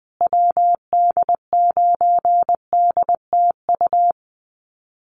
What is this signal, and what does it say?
Morse code, heard as a keyed tone, about 20 wpm, 700 Hz. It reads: WD9DTU
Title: USA Call Signs Repeated 3x 20wpm